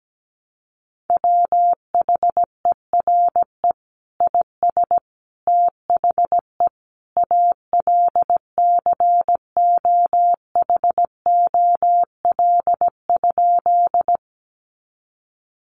Morse code, heard as a keyed tone, about 17 wpm, 700 Hz. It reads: WHERE IS THE ALCOHOL?
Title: Sentences from Top 1400 Words 17wpm